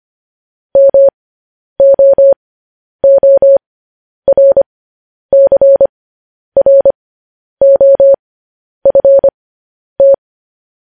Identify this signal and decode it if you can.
Morse code, heard as a keyed tone, about 25 wpm, spaced as for 10 wpm, 550 Hz. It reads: MOORCROFT
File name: podcast / News headlines-1x-25x10